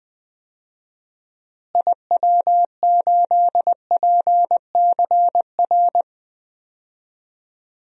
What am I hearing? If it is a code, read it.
Morse code, heard as a keyed tone, about 21 wpm, 700 Hz. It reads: IW8PCR
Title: DXCC-100 Call Signs - 6 Characters - Repeated 3x - 20wpm